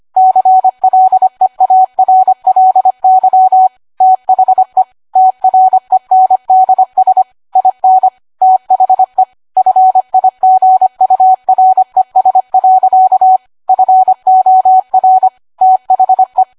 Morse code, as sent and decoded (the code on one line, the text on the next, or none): -.-. .-.. .- .-. .-.. -.-- / - .... / - .-. -. -.. ... / .. -. / - .... / ..-. .. --. ..- .-. ... .-.-.- / ..-. --- .-. / - .... .
CLEARLY THE TRENDS IN THE FIGURES. FOR THE